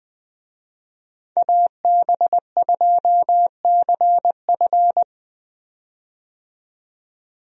Morse code, heard as a keyed tone, about 20 wpm, 700 Hz. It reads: AB2CF
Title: USA Call Signs - 5 Characters - 20wpm